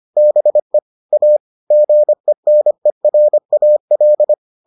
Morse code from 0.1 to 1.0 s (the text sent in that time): BE